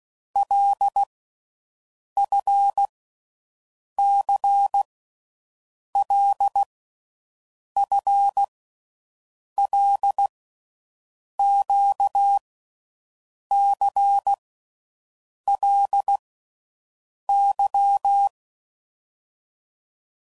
Morse code, heard as a keyed tone, about 16 wpm, 800 Hz. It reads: L F C L F L Q C L Y
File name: Groups / Basic Letter Groups / LFCYQ